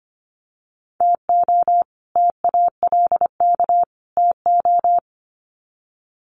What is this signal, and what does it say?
Morse code, heard as a keyed tone, about 25 wpm, 700 Hz. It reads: TO TALK TO
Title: Sets of 3 Words 25wpm